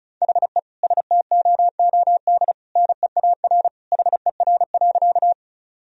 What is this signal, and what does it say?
Morse code, heard as a keyed tone, about 36 wpm, 700 Hz. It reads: HE STOOD NEAR HER.
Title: Sentences from Top 1000 Words - Review Part 2 - 35wpm